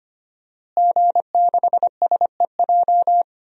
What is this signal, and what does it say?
Morse code, heard as a keyed tone, about 25 wpm, 700 Hz. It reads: G6SEJ
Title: DXCC-100 Call Signs - 5 Characters - Repeated 2x - 25wpm